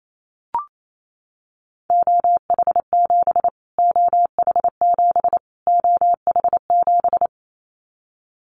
Morse code, heard as a keyed tone, about 28 wpm, 700 Hz. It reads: OH7 OH7 OH7
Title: Callsign Prefixes for Top 25 DXCC - Repeated 3x - 28wpm